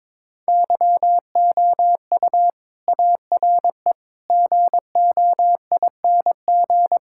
Morse code, heard as a keyed tone, about 21 wpm, 700 Hz. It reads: YOU ARE GOING
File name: practice / 22wpm / Sentences with 4-Word Phrases - 22wpm